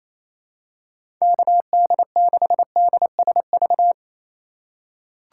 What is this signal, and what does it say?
Morse code, heard as a keyed tone, about 28 wpm, 700 Hz. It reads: KD6DSV
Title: USA Call Signs - 6 Characters - 3x Character Spacing - 28wpm